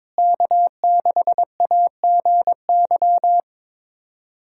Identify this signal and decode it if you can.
Morse code, heard as a keyed tone, about 22 wpm, 700 Hz. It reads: K6AGY
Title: USA Callsign DE Callsign 22wpm